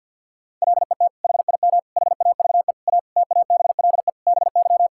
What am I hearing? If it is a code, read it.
Morse code, heard as a keyed tone, about 50 wpm, 700 Hz. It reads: LET HIM HAVE A TABLE BY
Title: Sentences from Top 1600 Words 50wpm